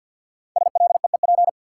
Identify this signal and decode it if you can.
Morse code, heard as a keyed tone, about 53 wpm, 700 Hz. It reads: SLEEP
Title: Sentences from Top 500 Words - New Word Drill - 50wpm